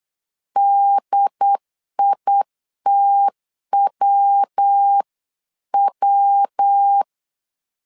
Morse code, heard as a keyed tone, about 8 wpm, 800 Hz. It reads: DITWW
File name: LEZIONI / 3 MWRO / 3 MW 8wpm 2501